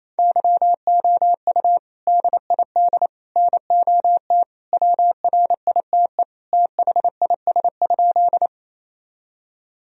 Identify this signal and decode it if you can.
Morse code, heard as a keyed tone, about 27 wpm, 700 Hz. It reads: YOU DID NOT WRITE THIS?
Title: Sentences from Top 1000 Words - Review Part 1 - 28wpm